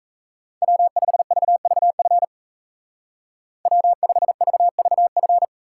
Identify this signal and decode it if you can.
Morse code, heard as a keyed tone, about 33 wpm, 700 Hz. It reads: W5VVF W5VVF
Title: Speed Racing - USA 1x3 Call Signs 28wpm